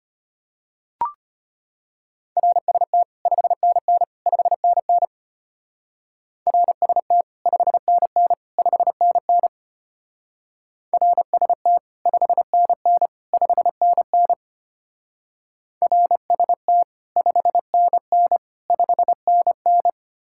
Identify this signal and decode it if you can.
Morse code, heard as a keyed tone, about 30 wpm, 700 Hz. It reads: RST 5NN 5NN RST 5NN 5NN RST 5NN 5NN RST 5NN 5NN
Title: Speed-Racing Common QSO Phrases Encore - 25wpm